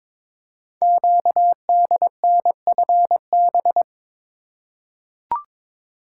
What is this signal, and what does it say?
Morse code, heard as a keyed tone, about 22 wpm, 700 Hz. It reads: QDNFB
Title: Code Groups - 5 Letters - 22wpm